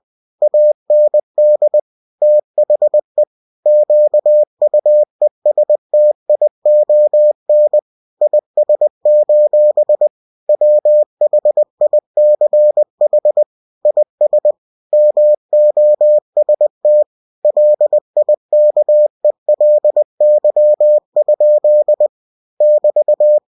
Morse code, heard as a keyed tone, about 20 wpm, 600 Hz. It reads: AND THE QUESTION IS: WHICH IS MOST LIKELY? <BT>
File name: cw-books / winnie-the-pooh / winnie-20wpm-page-0005